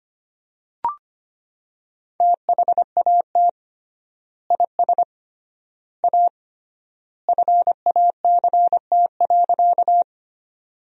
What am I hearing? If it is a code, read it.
Morse code, heard as a keyed tone, about 25 wpm, 700 Hz. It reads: THAT IS A FACT.
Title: 3x Word-Spacing - Sentences from Top 500 Words - 25wpm